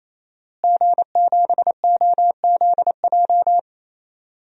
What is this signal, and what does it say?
Morse code, heard as a keyed tone, about 28 wpm, 700 Hz. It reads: G7OZJ
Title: DXCC-100 Call Signs - 5 Characters - Repeated 3x - 28wpm